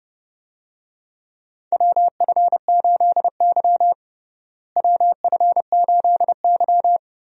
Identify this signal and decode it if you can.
Morse code, heard as a keyed tone, about 30 wpm, 700 Hz. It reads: WF8Y WF8Y
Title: USA Call Signs - 4 Characters - Repeated 2x - 30wpm